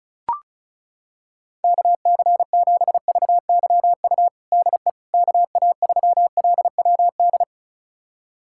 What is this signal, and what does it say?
Morse code, heard as a keyed tone, about 36 wpm, 700 Hz. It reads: KC7VYU DE KA3LWD
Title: USA Callsign DE Callsign 35wpm